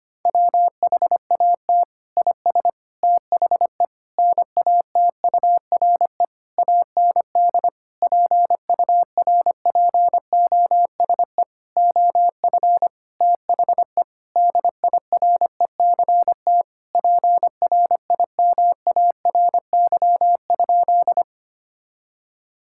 Morse code, heard as a keyed tone, about 25 wpm, 700 Hz. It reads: WHAT IS THE NATURE AND PURPOSE OF THE DIRECT PRIMARY?
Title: Sentences from Top 1000 Words - Review Part 5 - 25wpm